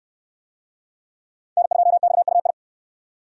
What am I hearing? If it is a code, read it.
Morse code, heard as a keyed tone, about 68 wpm, 700 Hz. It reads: N2XUI